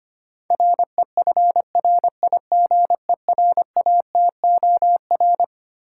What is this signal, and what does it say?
Morse code, heard as a keyed tone, about 25 wpm, 700 Hz. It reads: REFRIGERATOR
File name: practice / 25wpm / Easy Words with Hints - 25wpm